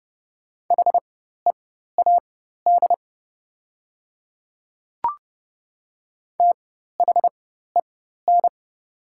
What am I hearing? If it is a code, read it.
Morse code, heard as a keyed tone, about 30 wpm, spaced as for 14 wpm, 700 Hz. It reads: HEAD THEN
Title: 4x Character Spacing - Top 300 Words - 30wpm